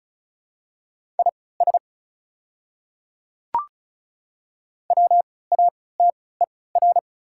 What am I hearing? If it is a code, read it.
Morse code, heard as a keyed tone, about 36 wpm, spaced as for 20 wpm, 690 Hz. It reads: IS WATER
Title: 3x Character Spacing - Top 100 Words - 35wpm